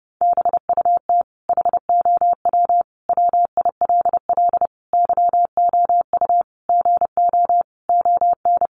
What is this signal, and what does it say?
Morse code, heard as a keyed tone, about 30 wpm, 700 Hz. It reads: BUT HOW WILL YOU GO ON
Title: Sentences from Top 1100 Words 30wpm